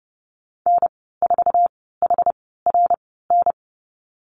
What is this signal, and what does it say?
Morse code, heard as a keyed tone, about 29 wpm, spaced as for 17 wpm, 700 Hz. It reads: N4HRN
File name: Lesson 15g - Number 5 - Call Signs 30wpm